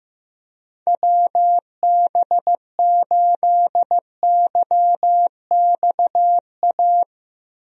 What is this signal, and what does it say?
Morse code, heard as a keyed tone, about 16 wpm, 700 Hz. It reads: WB8YXA
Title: USA Call Signs - 6 Characters - Repeated 2x - 15wpm